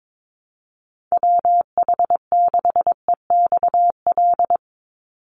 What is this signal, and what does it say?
Morse code, heard as a keyed tone, about 22 wpm, 700 Hz. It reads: WH6EXL
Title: USA Call Signs 22wpm